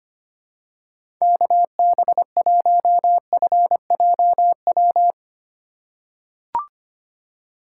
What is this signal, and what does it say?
Morse code, heard as a keyed tone, about 25 wpm, 700 Hz. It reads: KB1FJW
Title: USA Call Signs - 6 Characters - 2x Character Spacing - 25wpm